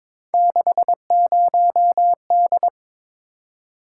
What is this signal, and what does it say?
Morse code, heard as a keyed tone, about 22 wpm, 700 Hz. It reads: 60D
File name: practice / 22wpm / Three Letter-Number Combinations - 22wpm